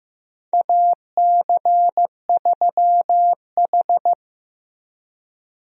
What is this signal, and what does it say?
Morse code, heard as a keyed tone, about 16 wpm, 700 Hz. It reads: AC3H
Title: USA Call Signs - 4 Characters - Repeated 2x - 15wpm